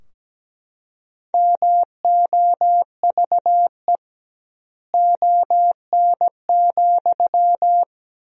Morse code, heard as a keyed tone, about 17 wpm, 700 Hz. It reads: MOVE ON,
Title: 2x Word-Spacing - Sentences from Top 500 Words - 17wpm